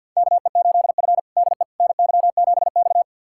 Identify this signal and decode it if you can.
Morse code, heard as a keyed tone, about 52 wpm, 700 Hz. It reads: KE8F DE NY7X
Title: USA Callsign DE Callsign - 4 Characters - 50wpm